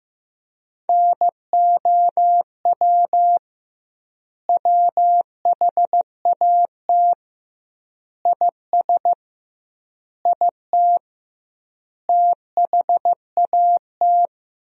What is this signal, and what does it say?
Morse code, heard as a keyed tone, about 15 wpm, 700 Hz. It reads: NOW WHAT IS IT THAT
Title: 2x Word-Spacing - Sentences from Top 200 Words - 15wpm